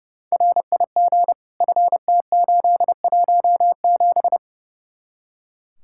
Morse code, heard as a keyed tone, about 36 wpm, 700 Hz. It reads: RIG FT817
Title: Speed-Racing Common QSO Phrases Encore - 30wpm